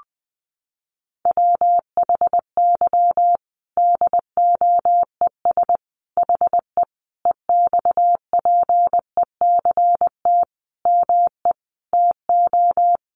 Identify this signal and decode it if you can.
Morse code, heard as a keyed tone, about 20 wpm, 700 Hz. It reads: WHY DOES HE EXPECT ME TO
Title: Sentences from Top 1100 Words 20wpm